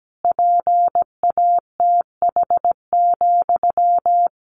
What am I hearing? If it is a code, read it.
Morse code, heard as a keyed tone, about 17 wpm, 700 Hz. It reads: PATH,
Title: A Sound Of Thunder 17wpm